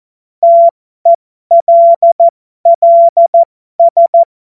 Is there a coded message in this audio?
Yes: Morse code from 0.4 to 4.2 s, heard as a keyed tone, about 14 wpm, spaced as for 12 wpm, 680 Hz. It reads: TELLS